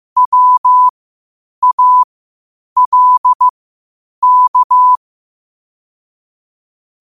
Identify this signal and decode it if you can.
Morse code, heard as a keyed tone, about 15 wpm, spaced as for 8 wpm, 1000 Hz. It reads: WALK
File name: qotdfiles / 8wpm